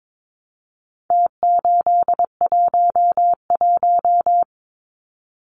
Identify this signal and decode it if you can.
Morse code, heard as a keyed tone, about 22 wpm, 700 Hz. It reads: T811
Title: Numbers - Contest Style - 4-Digit 22wpm